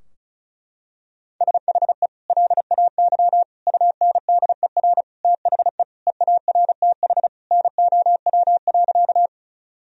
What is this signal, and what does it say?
Morse code, heard as a keyed tone, about 36 wpm, 700 Hz. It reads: SHE LAY UNDER THE EARTH NOW.
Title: Sentences from Top 1000 Words - Review Part 2 - 35wpm